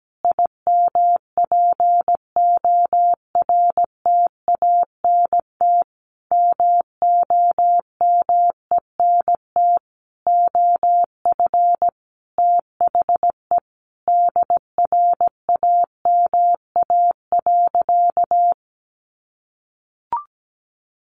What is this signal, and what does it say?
Morse code, heard as a keyed tone, about 17 wpm, 700 Hz. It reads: IMPORTANT MOMENT OF THE DRAMA.
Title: Sentences from Top 1500 Words 17wpm